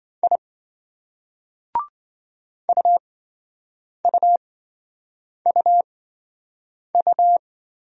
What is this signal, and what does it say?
Morse code, heard as a keyed tone, about 27 wpm, spaced as for 7 wpm, 700 Hz. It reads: I UUUU